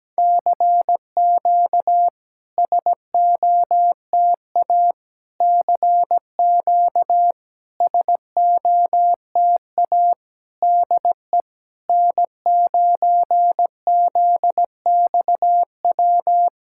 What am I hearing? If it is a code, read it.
Morse code, heard as a keyed tone, about 17 wpm, 700 Hz. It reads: CQ SOTA CQ SOTA DE N9ZXW
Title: SOTA QSO Practice 17wpm